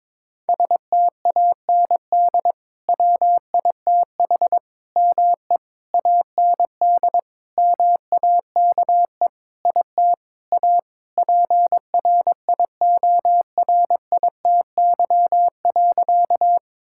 Morse code, heard as a keyed tone, about 22 wpm, 700 Hz. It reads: STAND WITH ME AND MAKE IT A PRIORITY.
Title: Sentences from Top 1600 Words 22wpm